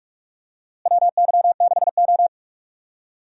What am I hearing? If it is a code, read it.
Morse code, heard as a keyed tone, about 46 wpm, 700 Hz. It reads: WY6O
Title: USA Call Signs - 4 Characters - Repeated 2x - 45wpm